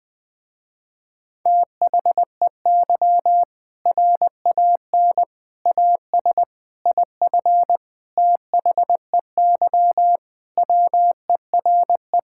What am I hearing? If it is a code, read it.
Morse code, heard as a keyed tone, about 21 wpm, 700 Hz. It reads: THEY RAN AS IF THEY WERE
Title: Sentences with 4-Word Phrases - Spoken Beginning - 20wpm